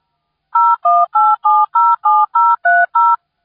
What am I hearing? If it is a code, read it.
DTMF, heard as telephone touch tones, tones of about 220 ms. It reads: 018707030